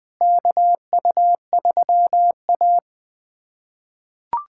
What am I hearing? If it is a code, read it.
Morse code, heard as a keyed tone, about 20 wpm, 700 Hz. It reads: KU3A
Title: USA Callsign DE Callsign - 4 Characters - 20wpm